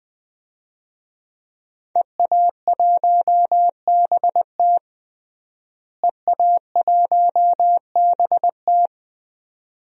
Morse code, heard as a keyed tone, about 20 wpm, 700 Hz. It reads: EA1BT EA1BT
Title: DXCC-100 Call Signs - 5 Characters - Repeated 2x - 20wpm